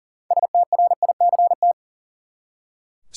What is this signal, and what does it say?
Morse code, heard as a keyed tone, about 40 wpm, 700 Hz. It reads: STRICT
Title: Six Letter Words - 40wpm